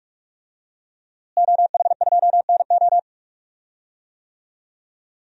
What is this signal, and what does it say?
Morse code, heard as a keyed tone, about 46 wpm, 700 Hz. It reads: OH2NO
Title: DXCC-100 Call Signs - 5 Characters - Repeated 2x - 45wpm